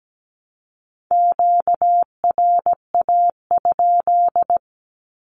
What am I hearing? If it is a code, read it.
Morse code, heard as a keyed tone, about 17 wpm, 700 Hz. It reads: QRA?
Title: Q-Codes Repeated 3x 17wpm